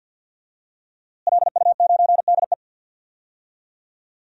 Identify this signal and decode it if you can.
Morse code, heard as a keyed tone, about 47 wpm, 700 Hz. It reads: LU9DE
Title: DXCC-100 Call Signs - 5 Characters - Repeated 3x - 50wpm